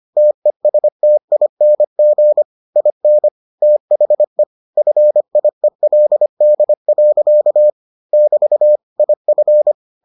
Morse code, heard as a keyed tone, about 25 wpm, 600 Hz. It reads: TESTING IN THE FIELD. <BT> IF